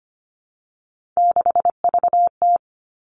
Morse code, heard as a keyed tone, about 25 wpm, 700 Hz. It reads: 6VT
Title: Three Challenging Letter-Number Combinations 25wpm